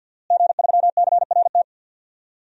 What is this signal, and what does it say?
Morse code, heard as a keyed tone, about 47 wpm, 700 Hz. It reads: G3CRT
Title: DXCC-100 Call Signs - 5 Characters - Repeated 2x - 50wpm